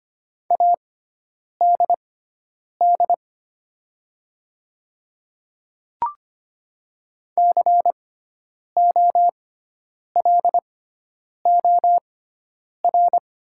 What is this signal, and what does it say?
Morse code, heard as a keyed tone, about 25 wpm, spaced as for 9 wpm, 700 Hz. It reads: ADD COLOR
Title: 6x Character Spacing - Top 500 Words - 25wpm